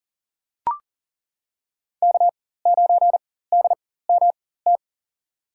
Morse code, heard as a keyed tone, about 40 wpm, spaced as for 19 wpm, 700 Hz. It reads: K9DMT
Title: USA Call Signs - 5 Characters - 4x Character Spacing - 40wpm